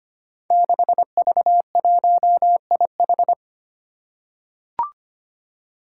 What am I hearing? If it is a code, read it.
Morse code, heard as a keyed tone, about 25 wpm, 700 Hz. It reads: 6V1IH
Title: Five Challenging Letter-Number Combinations 25wpm